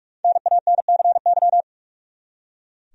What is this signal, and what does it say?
Morse code, heard as a keyed tone, about 42 wpm, 700 Hz. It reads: NANCY
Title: USA Common Names - Max 5 Letters - 45wpm